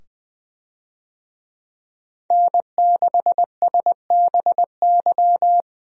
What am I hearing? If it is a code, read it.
Morse code, heard as a keyed tone, about 20 wpm, 700 Hz. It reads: N6SBY
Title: USA Call Signs - 5 Characters - 20wpm